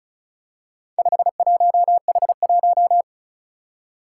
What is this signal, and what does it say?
Morse code, heard as a keyed tone, about 36 wpm, 690 Hz. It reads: 51H1
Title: Four Challenging Letter-Number Combinations 35wpm